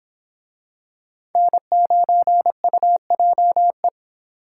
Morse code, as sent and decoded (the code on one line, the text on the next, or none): -. ----. ..- .--- .
N9UJE